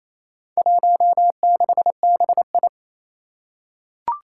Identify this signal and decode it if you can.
Morse code, heard as a keyed tone, about 28 wpm, 700 Hz. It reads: 16BI
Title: Four Challenging Letter-Number Combinations 28wpm